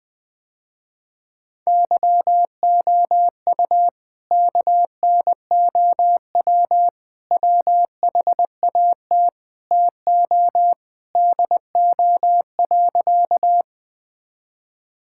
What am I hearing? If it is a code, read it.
Morse code, heard as a keyed tone, about 21 wpm, 700 Hz. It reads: YOU KNOW WHAT TO DO.
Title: Sentences with 3-Word Phrases - Spoken Beginning - 20wpm